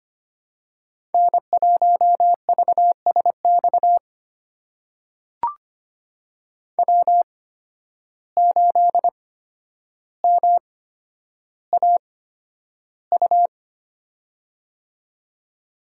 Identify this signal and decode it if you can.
Morse code, heard as a keyed tone, about 25 wpm, spaced as for 11 wpm, 700 Hz. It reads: N1VSX W8MAU
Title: USA Call Signs - 5 Characters - 8x Character Spacing - 25wpm